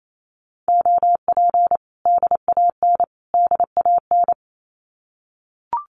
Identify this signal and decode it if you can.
Morse code, heard as a keyed tone, about 28 wpm, 700 Hz. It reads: OP DAN DAN
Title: Common QSO Phrases Encore 28wpm